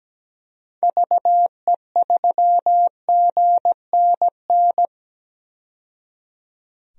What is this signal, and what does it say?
Morse code, heard as a keyed tone, about 17 wpm, 700 Hz. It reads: VE3GNN
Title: DXCC-100 Call Signs - 6 Characters - 17wpm